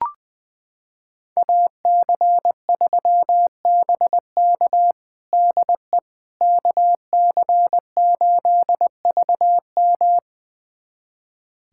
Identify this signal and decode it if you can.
Morse code, heard as a keyed tone, about 21 wpm, 700 Hz. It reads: AC3BK DE KC8VM
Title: USA Callsign DE Callsign - 5 Characters - 20wpm